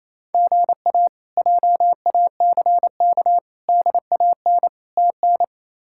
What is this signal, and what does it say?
Morse code, heard as a keyed tone, about 28 wpm, 700 Hz. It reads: GA JACK DAN TN
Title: K1USN SST QSO Practice - 28wpm